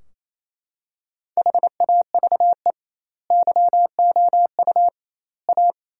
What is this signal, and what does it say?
Morse code, heard as a keyed tone, about 28 wpm, 700 Hz. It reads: HAVE YOU A
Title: 2x Word-Spacing - Sentences from Top 400 Words - 28wpm